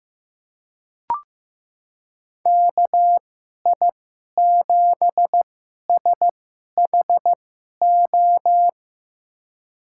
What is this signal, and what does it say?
Morse code, heard as a keyed tone, about 16 wpm, spaced as for 11 wpm, 700 Hz. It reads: KI7SHO
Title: USA Call Signs - 6 Characters - 2x Character Spacing - 15wpm